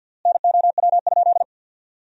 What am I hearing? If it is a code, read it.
Morse code, heard as a keyed tone, about 50 wpm, 700 Hz. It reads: NOW?
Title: 4x Word-Spacing - Sentences from Top 100 Words - 50wpm